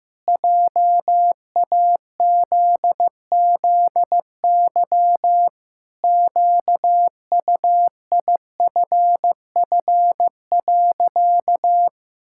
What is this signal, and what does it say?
Morse code, heard as a keyed tone, about 15 wpm, 700 Hz. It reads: JAZZY QUIFF.
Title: Pangram Challenge v2 - 15wpm